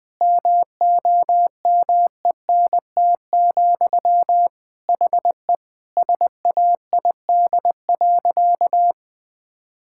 Morse code, heard as a keyed tone, about 20 wpm, 700 Hz. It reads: MOMENT, HE SAID.
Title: Sentences from Top 1000 Words - Review Part 5 - 20wpm